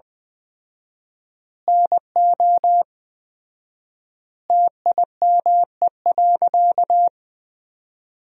Morse code, heard as a keyed tone, about 20 wpm, 700 Hz. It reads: NO TIME.
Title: 4x Word-Spacing - Sentences from Top 100 Words - 20wpm